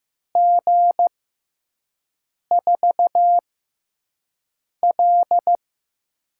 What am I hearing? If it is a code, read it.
Morse code, heard as a keyed tone, about 15 wpm, spaced as for 5 wpm, 700 Hz. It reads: G4L
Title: USA Call Signs - 4 Characters - 6x Character Spacing - 15wpm